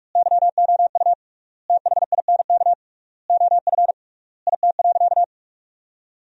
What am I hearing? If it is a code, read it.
Morse code, heard as a keyed tone, about 46 wpm, 700 Hz. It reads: YOU THINK OF IT.
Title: 3x Word-Spacing - Sentences from Top 200 Words - 45wpm